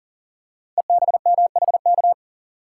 Morse code, heard as a keyed tone, about 39 wpm, 700 Hz. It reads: EBMHK
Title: Code Groups - 5 Letters - 40wpm